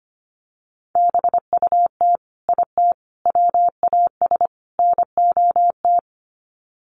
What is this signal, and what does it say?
Morse code, heard as a keyed tone, about 25 wpm, 700 Hz. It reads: BUT IT WAS NOT
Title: Sets of 4 Words 25wpm